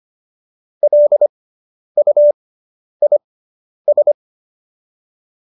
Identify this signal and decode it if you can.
Morse code, heard as a keyed tone, about 25 wpm, spaced as for 10 wpm, 570 Hz. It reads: LUIS